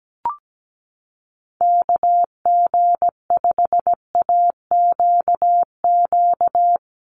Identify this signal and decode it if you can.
Morse code, heard as a keyed tone, about 17 wpm, 700 Hz. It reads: KG5AQQ
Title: USA Call Signs Repeated 3x 17wpm